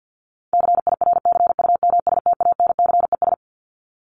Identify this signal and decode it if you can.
Morse code, heard as a keyed tone, about 50 wpm, 700 Hz. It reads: CIRCUMSTANCES
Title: ES:S Suffix - Top 100 Words with Base Words 50wpm